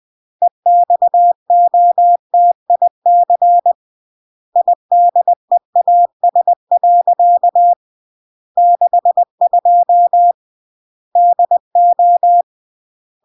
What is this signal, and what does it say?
Morse code, heard as a keyed tone, about 20 wpm, 700 Hz. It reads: EXOTIC IDEAS. 62 DO
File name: CW / jodorowsky / 82 commandments 20-20